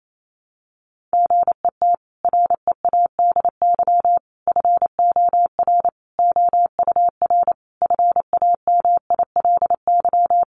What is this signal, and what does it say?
Morse code, heard as a keyed tone, about 28 wpm, 700 Hz. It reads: GET READY FOR OUR FAMILY
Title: Sentences from Top 1200 Words 28wpm